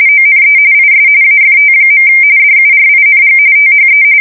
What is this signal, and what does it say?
RTTY, about 45.45 baud. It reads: ONE IS AFT. DURING THE
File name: fsk170 / battleshipnj